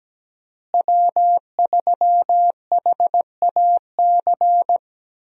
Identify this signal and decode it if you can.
Morse code, heard as a keyed tone, about 17 wpm, 700 Hz. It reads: W3HAC
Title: USA Call Signs - 5 Characters - Repeated 3x - 17wpm